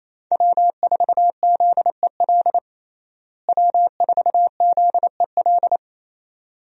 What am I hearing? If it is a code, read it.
Morse code, heard as a keyed tone, about 28 wpm, 700 Hz. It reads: W4ZEL W4ZEL
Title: USA Call Signs - 5 Characters - Repeated 2x - 28wpm